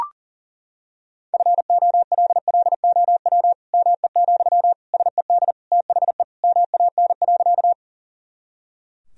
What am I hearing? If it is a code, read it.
Morse code, heard as a keyed tone, about 37 wpm, 700 Hz. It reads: FOLLOW ME, SED THE MAN.
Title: Sentences from Top 200 Words and Abbreviations 40wpm